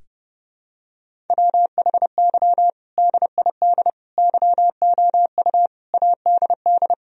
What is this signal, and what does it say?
Morse code, heard as a keyed tone, about 30 wpm, 700 Hz. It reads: WHY DID YOU ADD
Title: Sentences from Top 1000 Words - Review Part 1 - 30wpm